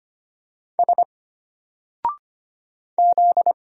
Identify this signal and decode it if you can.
Morse code, heard as a keyed tone, about 25 wpm, 700 Hz. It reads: S Z
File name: Single-Letter-Number-25wpm